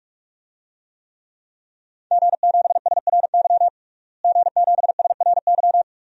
Morse code, heard as a keyed tone, about 47 wpm, 700 Hz. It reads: G7SRY G7SRY
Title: DXCC-100 Call Signs - 5 Characters - Repeated 2x - 45wpm